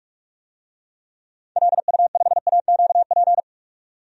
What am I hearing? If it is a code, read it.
Morse code, heard as a keyed tone, about 47 wpm, 700 Hz. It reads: LU5AQP
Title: DXCC-100 Call Signs - 6 Characters - 45wpm